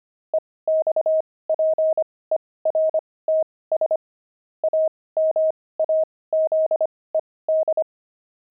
Morse code, heard as a keyed tone, about 25 wpm, spaced as for 18 wpm, 630 Hz. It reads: EXPERTS AMAZED
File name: pangram-fs-18-cs-25